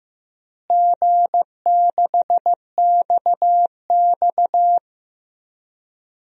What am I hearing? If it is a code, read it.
Morse code, heard as a keyed tone, about 15 wpm, 700 Hz. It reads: G6XX
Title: DXCC-100 Call Signs - 4 Characters - Repeated 2x - 15wpm